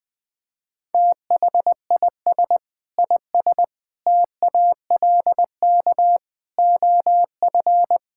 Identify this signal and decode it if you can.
Morse code, heard as a keyed tone, about 20 wpm, 700 Hz. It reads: THIS IS TALK OF
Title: Build A Phrase - 5-Word Phrases - 20wpm